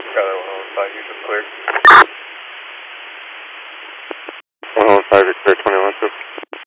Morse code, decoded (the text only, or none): none